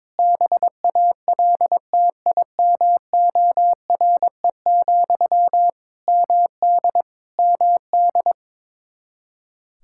Morse code, heard as a keyed tone, about 22 wpm, 700 Hz. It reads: BALTIMORE, MD MD
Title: Practice QSO - Exchange 1 - Beginning - 22wpm